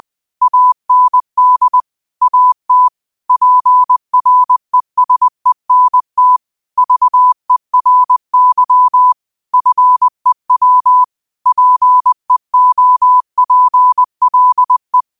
Morse code, heard as a keyed tone, about 20 wpm, 1000 Hz. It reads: AND AT PRESENT VERY FEW PEOPLE
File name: qotdfiles / 20wpm